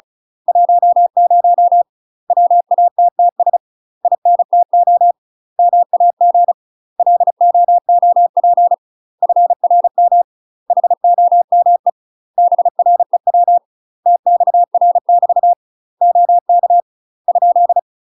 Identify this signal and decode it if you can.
Morse code, heard as a keyed tone, about 36 wpm, 700 Hz. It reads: 10 WATTS INTO MAG LOOP FRM HOME BREW TXR <BT> OK ?
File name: UK MASTER QSOs TRAINING / 35WPM / UK MASTER QSOs 0011